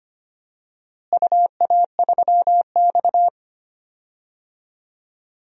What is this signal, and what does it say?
Morse code, heard as a keyed tone, about 25 wpm, 700 Hz. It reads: UA3X
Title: DXCC-100 Call Signs - 4 Characters - Repeated 2x - 25wpm